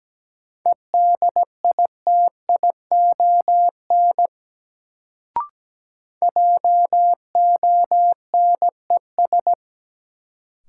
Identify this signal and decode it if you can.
Morse code, heard as a keyed tone, about 17 wpm, 700 Hz. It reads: EDITION JONES
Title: Sets of 5 Words - New Word Drill - 17wpm